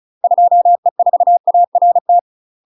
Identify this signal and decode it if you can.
Morse code, heard as a keyed tone, about 36 wpm, 700 Hz. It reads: 2E4ART
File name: UK MASTER QSOs TRAINING / 35WPM / UK MASTER QSOs 0023